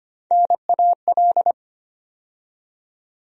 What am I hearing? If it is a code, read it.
Morse code, heard as a keyed tone, about 25 wpm, 700 Hz. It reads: NAL
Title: USA Call Sign Postfixes - 3 Letters 25wpm